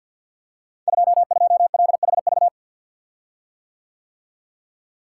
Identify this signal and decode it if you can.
Morse code, heard as a keyed tone, about 51 wpm, 700 Hz. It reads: 22LHV
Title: Code Groups - 5 Letter:Numbers - 50wpm